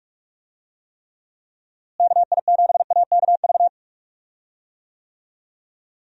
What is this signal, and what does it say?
Morse code, heard as a keyed tone, about 46 wpm, 700 Hz. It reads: KI7AKV
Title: USA Call Signs - 6 Characters - Repeated 2x - 45wpm